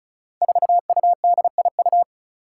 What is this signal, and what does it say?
Morse code, heard as a keyed tone, about 36 wpm, 700 Hz. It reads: 4UDIU
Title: Five Challenging Letter-Number Combinations 35wpm